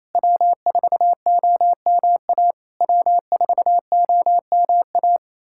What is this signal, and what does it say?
Morse code, heard as a keyed tone, about 28 wpm, 690 Hz. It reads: W4OMA W4OMA
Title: SOTA QSO Practice 28wpm